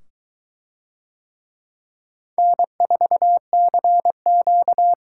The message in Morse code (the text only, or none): N4CQ